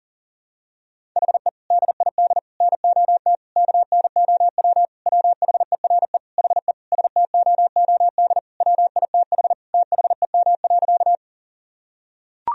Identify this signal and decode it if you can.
Morse code, heard as a keyed tone, about 40 wpm, 700 Hz. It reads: HE DID NOT KNOW WHERE HE STOOD WITH THEM.
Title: Sentences from Top 1000 Words - Review Part 2 - 40wpm